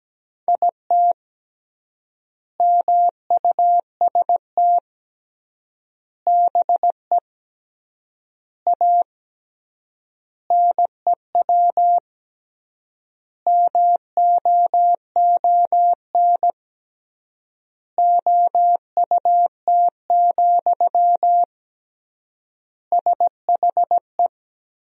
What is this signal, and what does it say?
Morse code, heard as a keyed tone, about 17 wpm, 700 Hz. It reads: IT MUST BE A NEW MOON OUT, SHE
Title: 3x Word-Spacing - Sentences from Top 500 Words - 17wpm